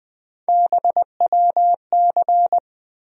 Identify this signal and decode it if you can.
Morse code, heard as a keyed tone, about 20 wpm, 700 Hz. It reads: BWC
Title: USA Call Sign Postfixes - 3 Letters 20wpm